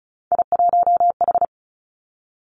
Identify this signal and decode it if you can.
Morse code, heard as a keyed tone, about 36 wpm, 700 Hz. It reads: I1H
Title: Three Challenging Letter-Number Combinations 35wpm